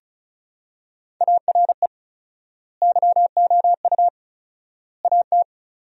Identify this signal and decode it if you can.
Morse code, heard as a keyed tone, about 36 wpm, 700 Hz. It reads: ARE YOU AT
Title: 4x Word-Spacing - Sentences from Top 200 Words - 35wpm